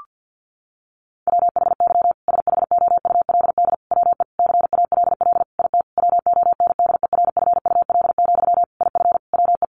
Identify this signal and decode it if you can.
Morse code, heard as a keyed tone, about 50 wpm, 700 Hz. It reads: WHY SHOULD WE CALL IT WONDERFUL, IF WE
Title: Sentences from Top 1100 Words 50wpm